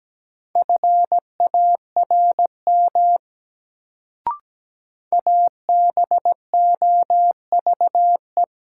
Morse code, heard as a keyed tone, about 17 wpm, 700 Hz. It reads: FARM ABOVE
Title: Sentences from Top 1000 Words - Review Part 2 - New Word Drill - 17wpm